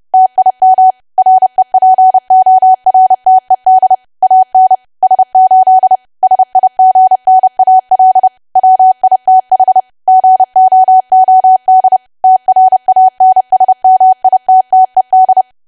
Morse code, heard as a keyed tone, about 30 wpm, 750 Hz. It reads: TIM REPORTED AN S8 SIGNAL WITH GOOD TRANSMITTED
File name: cwa / W1AW-Level3 / 30WPM